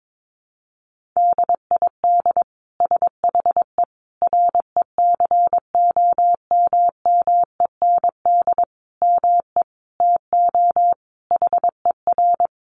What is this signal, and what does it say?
Morse code, heard as a keyed tone, about 22 wpm, 700 Hz. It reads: DID SHE RECOMMEND ME TO HER
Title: Sentences from Top 1200 Words 22wpm